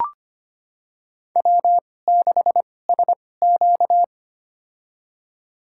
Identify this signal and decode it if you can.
Morse code, heard as a keyed tone, about 25 wpm, spaced as for 18 wpm, 700 Hz. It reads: W6SQ
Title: USA Call Signs - 4 Characters - 2x Character Spacing - 25wpm